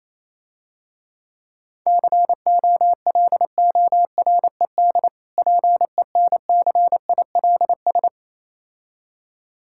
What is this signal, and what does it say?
Morse code, heard as a keyed tone, about 27 wpm, 700 Hz. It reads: COLORED PENCILS
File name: Easy Words - Speed-Racing with Hints - 22wpm